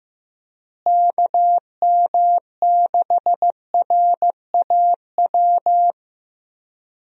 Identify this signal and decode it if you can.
Morse code, heard as a keyed tone, about 15 wpm, 700 Hz. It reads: KM6RAW
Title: USA Call Signs - 6 Characters - Repeated 3x - 15wpm